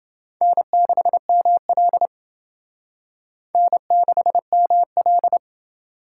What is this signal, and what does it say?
Morse code, heard as a keyed tone, about 28 wpm, 700 Hz. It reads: N6ML N6ML